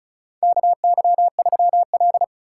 Morse code, heard as a keyed tone, about 36 wpm, 700 Hz. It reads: KY3L
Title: USA Call Signs - 4 Characters - 4x Character Spacing - 35wpm